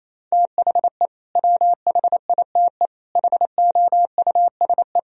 Morse code, heard as a keyed tone, about 28 wpm, 700 Hz. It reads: THE WHITE HOUSE
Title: Sentences from Top 1000 Words - Review Part 1 - 28wpm